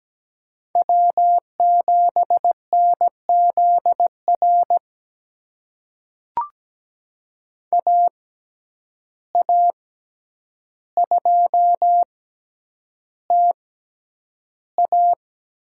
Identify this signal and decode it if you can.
Morse code, heard as a keyed tone, about 17 wpm, spaced as for 9 wpm, 700 Hz. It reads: W7NZR AA2TA
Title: USA Call Signs - 5 Characters - 6x Character Spacing - 17wpm